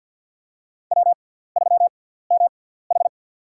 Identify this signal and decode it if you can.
Morse code, heard as a keyed tone, about 47 wpm, spaced as for 17 wpm, 700 Hz. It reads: W3MH
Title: USA Call Signs - 4 Characters - 6x Character Spacing - 50wpm